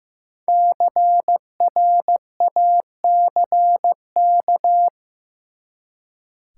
Morse code, heard as a keyed tone, about 15 wpm, 700 Hz. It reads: CRACK
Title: Five Letter Words - 15wpm